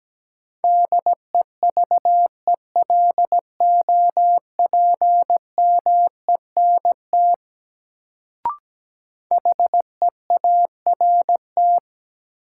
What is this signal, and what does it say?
Morse code, heard as a keyed tone, about 17 wpm, 700 Hz. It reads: DEVELOPMENT HEART
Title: Sets of 3 Words - New Word Drill - 17wpm